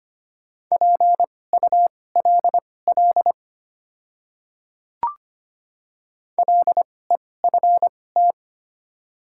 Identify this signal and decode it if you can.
Morse code, heard as a keyed tone, about 25 wpm, spaced as for 18 wpm, 700 Hz. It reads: PULL LEFT
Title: 2x Character Spacing - Top 500 Words - 25wpm